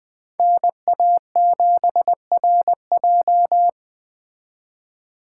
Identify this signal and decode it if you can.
Morse code, heard as a keyed tone, about 21 wpm, 700 Hz. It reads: NA7RJ
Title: Speed Racing - USA 2x2 Call Signs 17wpm